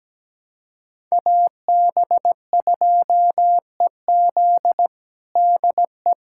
Morse code, heard as a keyed tone, about 17 wpm, 700 Hz. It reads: AB2EZ DE
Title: USA Callsign DE Callsign - 5 Characters - 17wpm